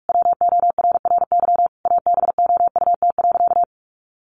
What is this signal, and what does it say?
Morse code, heard as a keyed tone, about 44 wpm, 700 Hz. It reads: WORRY ABOUT.
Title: Sentences from Top 1100 Words 45wpm